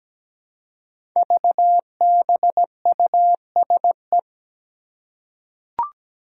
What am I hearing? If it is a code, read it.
Morse code, heard as a keyed tone, about 17 wpm, 700 Hz. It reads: VBUSE
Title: Five Challenging Letter-Number Combinations 17wpm